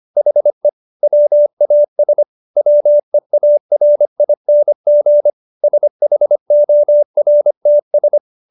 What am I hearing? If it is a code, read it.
Morse code, heard as a keyed tone, about 25 wpm, 590 Hz. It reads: HE WAS WEARING SHORTS